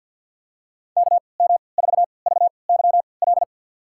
Morse code, heard as a keyed tone, about 48 wpm, spaced as for 28 wpm, 690 Hz. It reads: KM4VYL
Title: USA Call Signs - 6 Characters - 3x Character Spacing - 50wpm